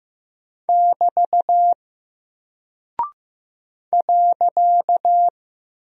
Morse code, heard as a keyed tone, about 15 wpm, 700 Hz. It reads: <BT> .
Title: Single Character (Letter:Number:Punctuation:Prosign) Alt 15wpm